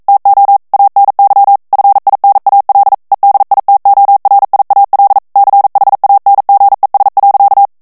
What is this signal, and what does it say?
Morse code, heard as a keyed tone, about 42 wpm, 800 Hz. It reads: TO ANY FINAL EDITORIAL CHANGES.